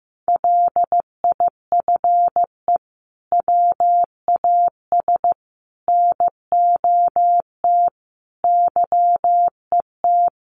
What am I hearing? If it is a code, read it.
Morse code, heard as a keyed tone, about 15 wpm, 700 Hz. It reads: LIFE WAS NOT YET
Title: Sentences from Top 1200 Words 15wpm